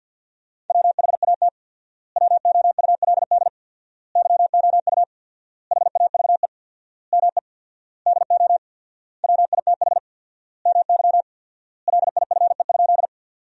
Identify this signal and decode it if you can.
Morse code, heard as a keyed tone, about 51 wpm, 700 Hz. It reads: WHAT WOULD YOU HAVE ME DO WITH MY LIFE?
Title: 4x Word-Spacing - Sentences from Top 300 Words - 50wpm